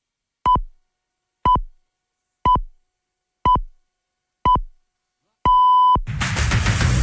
Radio time signal (time pips). It is heard from 0.4 to 6.0 s.